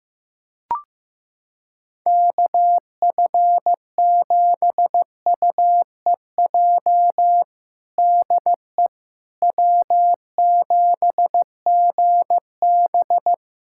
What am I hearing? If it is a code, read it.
Morse code, heard as a keyed tone, about 15 wpm, 700 Hz. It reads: KF7UEJ DE W7GB
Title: Practice QSO - Exchange 4 - Ending - 15wpm